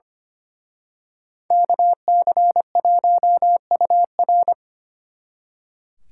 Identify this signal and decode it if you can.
Morse code, heard as a keyed tone, about 29 wpm, 700 Hz. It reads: KC1UR
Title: Speed Racing - USA Call Signs 25wpm